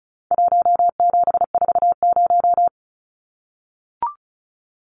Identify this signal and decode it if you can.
Morse code, heard as a keyed tone, about 34 wpm, 700 Hz. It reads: 1740
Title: Numbers - 4-Digit 35wpm